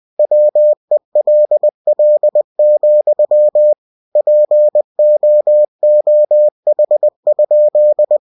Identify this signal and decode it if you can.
Morse code, heard as a keyed tone, about 20 wpm, 600 Hz. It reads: WELL, POOH?
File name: cw-books / winnie-the-pooh / winnie-20wpm-page-0065